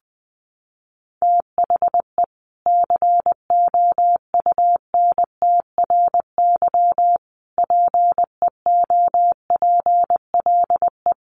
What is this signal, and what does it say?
Morse code, heard as a keyed tone, about 20 wpm, 700 Hz. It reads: THE COUNTRY PEOPLE
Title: Sentences from Top 1100 Words 20wpm